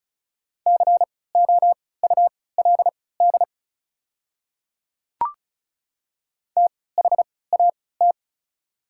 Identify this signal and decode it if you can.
Morse code, heard as a keyed tone, about 36 wpm, spaced as for 20 wpm, 690 Hz. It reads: COULD THAT
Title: 3x Character Spacing - Top 100 Words - 35wpm